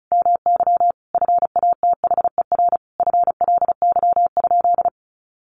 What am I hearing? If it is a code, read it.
Morse code, heard as a keyed tone, about 34 wpm, 700 Hz. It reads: MY FATHER FLY?
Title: Sentences from Top 500 Words 35wpm